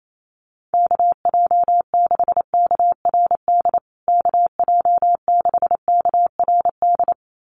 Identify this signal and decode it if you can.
Morse code, heard as a keyed tone, about 28 wpm, 700 Hz. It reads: KJ6KRD KJ6KRD
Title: USA Call Signs Repeated 2x 28wpm